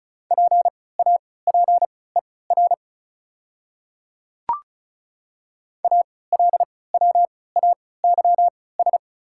Morse code, heard as a keyed tone, about 34 wpm, spaced as for 20 wpm, 690 Hz. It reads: PAPER ALWAYS
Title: 3x Character Spacing - Top 300 Words - 35wpm